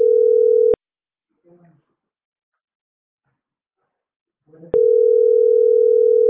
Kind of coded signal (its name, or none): telephone call progress tone